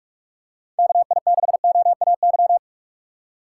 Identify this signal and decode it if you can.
Morse code, heard as a keyed tone, about 41 wpm, 700 Hz. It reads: KI6OAY